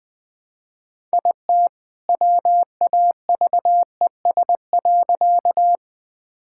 Morse code, heard as a keyed tone, about 20 wpm, 700 Hz. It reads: IT WAVES.
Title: All-Age Appropriate Jokes - 20wpm